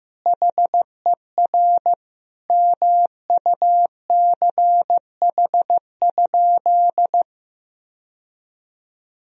Morse code, heard as a keyed tone, about 15 wpm, 700 Hz. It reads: HER MUCH?
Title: Sentences from Top 1000 Words - Review Part 1 - 15wpm